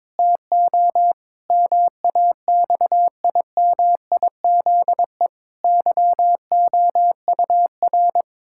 Morse code, heard as a keyed tone, about 21 wpm, 700 Hz. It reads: TO MAXIMIZE YOUR
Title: Pangram Challenge v2 - 22wpm